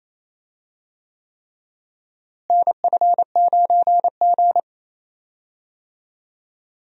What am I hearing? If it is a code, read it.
Morse code, heard as a keyed tone, about 28 wpm, 700 Hz. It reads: NF9G
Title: USA Call Signs - 4 Characters - Repeated 3x - 28wpm